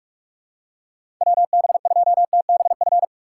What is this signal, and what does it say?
Morse code, heard as a keyed tone, about 44 wpm, 700 Hz. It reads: WB2TBF
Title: USA Call Signs - 6x Character Spacing - 45wpm